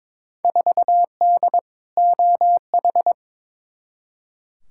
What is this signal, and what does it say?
Morse code, heard as a keyed tone, about 22 wpm, 700 Hz. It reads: 4D OH
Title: Field Day Exchanges Encore - 22wpm